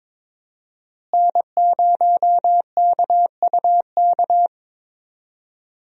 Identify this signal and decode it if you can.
Morse code, heard as a keyed tone, about 22 wpm, 700 Hz. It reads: N0KUK
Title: USA Call Signs - 5 Characters - Repeated 2x - 22wpm